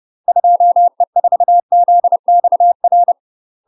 Morse code, heard as a keyed tone, about 30 wpm, 700 Hz. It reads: 2E4ZXR
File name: UK MASTER QSOs TRAINING / 30WPM / UK MASTER QSOs 0002